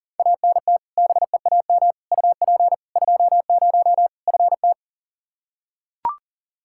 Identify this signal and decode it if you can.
Morse code, heard as a keyed tone, about 40 wpm, 700 Hz. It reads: ANT BEAM UP 20 FT
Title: K1USN SST QSO Practice 40wpm